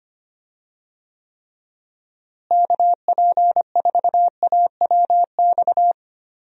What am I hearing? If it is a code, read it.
Morse code, heard as a keyed tone, about 25 wpm, 700 Hz. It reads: KP4AWX
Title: DXCC-100 Call Signs - 6 Characters - Repeated 3x - 25wpm